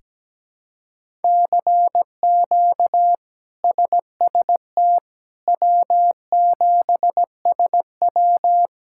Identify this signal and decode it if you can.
Morse code, heard as a keyed tone, about 17 wpm, 700 Hz. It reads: CQ SST W7SW
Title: K1USN SST QSO Practice - 17wpm